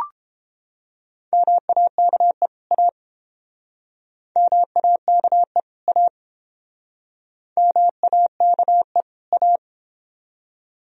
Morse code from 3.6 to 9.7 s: MAKE A MAKE A

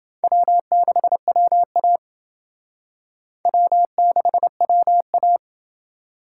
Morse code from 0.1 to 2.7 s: W6WA

